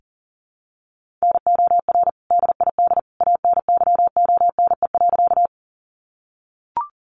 Morse code, heard as a keyed tone, about 40 wpm, 700 Hz. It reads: NOR DID ANYONE.
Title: Sentences from Top 1500 Words 40wpm